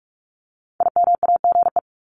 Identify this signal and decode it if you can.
Morse code, heard as a keyed tone, about 44 wpm, 700 Hz. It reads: IMAGE
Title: Top 501-750 Words 45wpm